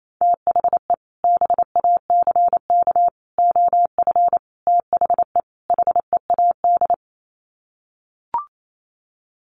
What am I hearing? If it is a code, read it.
Morse code, heard as a keyed tone, about 28 wpm, 700 Hz. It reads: THE BACK OF THE HEAD
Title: Sets of 5 Words Encore2 28wpm